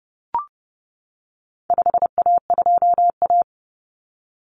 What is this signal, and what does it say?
Morse code, heard as a keyed tone, about 30 wpm, 700 Hz. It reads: 5A2A
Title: Common Cut Numbers - 4-Digit 30wpm